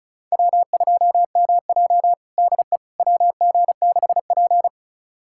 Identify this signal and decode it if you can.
Morse code, heard as a keyed tone, about 34 wpm, 700 Hz. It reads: W2MJ DE WG6P
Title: USA Callsign DE Callsign - 4 Characters - 35wpm